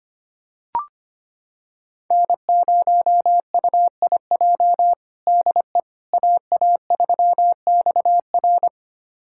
Morse code, heard as a keyed tone, about 25 wpm, 700 Hz. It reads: N0UIJ DE AA3XR
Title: USA Callsign DE Callsign - 5 Characters - 25wpm